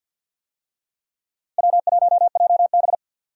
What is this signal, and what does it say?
Morse code, heard as a keyed tone, about 46 wpm, 700 Hz. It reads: W1JB
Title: USA Call Signs - 4 Characters - Repeated 2x - 50wpm